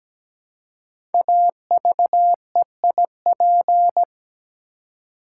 Morse code, heard as a keyed tone, about 17 wpm, 700 Hz. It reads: AVEIP
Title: Code Groups - 5 Letters - 17wpm